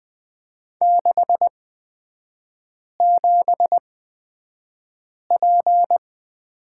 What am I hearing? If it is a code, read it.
Morse code, heard as a keyed tone, about 20 wpm, spaced as for 5 wpm, 700 Hz. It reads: 67P
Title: 255 Lesson 41e - Period - Three Characters 20wpm